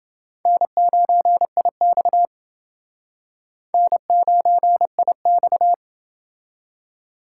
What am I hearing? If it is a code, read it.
Morse code, heard as a keyed tone, about 28 wpm, 700 Hz. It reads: N9IX N9IX